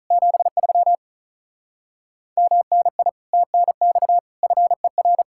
Morse code, heard as a keyed tone, about 33 wpm, 700 Hz. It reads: 73 MNI TNX FER
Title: Speed-Racing Common QSO Phrases - 35wpm